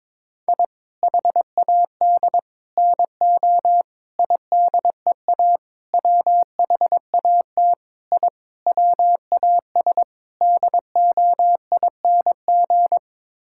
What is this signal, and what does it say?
Morse code, heard as a keyed tone, about 21 wpm, 700 Hz. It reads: I HAD NO IDEA WHAT I WAS DOING
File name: practice / 22wpm / Sets of 8 Words - 22wpm